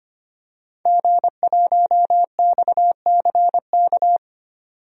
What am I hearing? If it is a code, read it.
Morse code, heard as a keyed tone, about 25 wpm, 700 Hz. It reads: G1XCK
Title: DXCC-100 Call Signs - 5 Characters - 25wpm